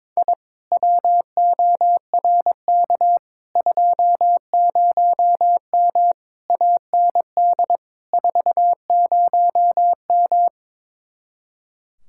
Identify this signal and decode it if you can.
Morse code, heard as a keyed tone, about 22 wpm, 700 Hz. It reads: I WORK 20M AND 40M
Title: Speed-Racing Common QSO Phrases Encore - 22wpm